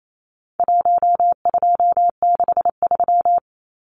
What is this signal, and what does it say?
Morse code, heard as a keyed tone, about 28 wpm, 700 Hz. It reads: 1263
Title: Numbers - 4-Digit 28wpm